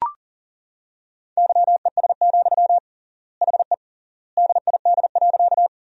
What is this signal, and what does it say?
Morse code, heard as a keyed tone, about 40 wpm, 700 Hz. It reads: YES, HE DID.
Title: 3x Word-Spacing - Sentences from Top 500 Words - 40wpm